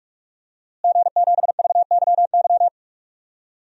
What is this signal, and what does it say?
Morse code, heard as a keyed tone, about 46 wpm, 700 Hz. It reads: G7VYY
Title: DXCC-100 Call Signs - 5 Characters - Repeated 2x - 45wpm